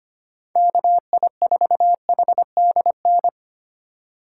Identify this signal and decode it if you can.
Morse code, heard as a keyed tone, about 25 wpm, 700 Hz. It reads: KI4HDN
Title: USA Call Signs - 6 Characters - 2x Character Spacing - 25wpm